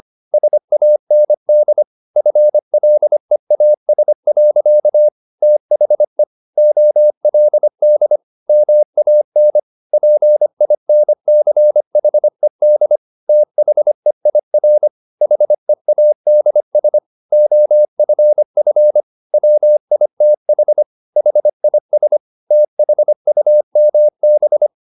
Morse code, heard as a keyed tone, about 25 wpm, 600 Hz. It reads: SAND FLEAS. THE OLD MAN PINCHED THEIR HEADS OFF WITH HIS THUMB